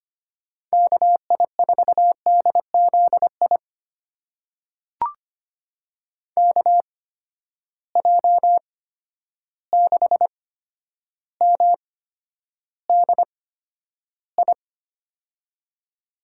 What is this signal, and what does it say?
Morse code, heard as a keyed tone, about 25 wpm, spaced as for 11 wpm, 700 Hz. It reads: KI4DZI KJ6MDI
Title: USA Call Signs - 6 Characters - 8x Character Spacing - 25wpm